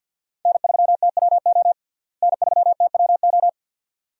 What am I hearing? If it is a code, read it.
Morse code, heard as a keyed tone, about 50 wpm, 700 Hz. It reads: N3TWO N3TWO
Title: USA Call Signs - 5 Characters - Repeated 2x - 50wpm